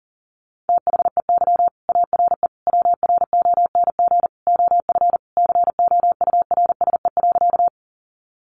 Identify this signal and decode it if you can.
Morse code, heard as a keyed tone, about 40 wpm, 700 Hz. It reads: THEY ARE WRONG OF COURSE.
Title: Sentences from Top 900 Words 40wpm